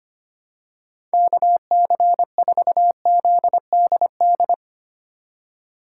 Morse code, heard as a keyed tone, about 25 wpm, 700 Hz. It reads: KC4ZDD
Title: USA Call Signs - 2x Character Spacing - 25wpm